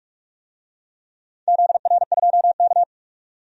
Morse code, heard as a keyed tone, about 44 wpm, 700 Hz. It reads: ZR2K
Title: DXCC-100 Call Signs - 4 Characters - 45wpm